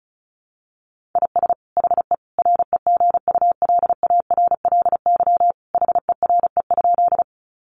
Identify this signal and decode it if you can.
Morse code, heard as a keyed tone, about 34 wpm, 700 Hz. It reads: IS HE REGULARLY HERE?
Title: Sentences from Top 1400 Words 35wpm